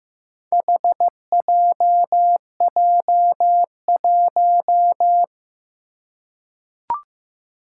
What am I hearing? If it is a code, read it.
Morse code, heard as a keyed tone, about 15 wpm, 700 Hz. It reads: HJJ1
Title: Four Challenging Letter-Number Combinations 15wpm